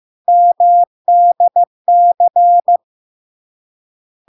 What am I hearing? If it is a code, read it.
Morse code, heard as a keyed tone, about 15 wpm, 700 Hz. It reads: MDC
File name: LARK-FD-2022-Log-in-CW-15wpm-700-Hz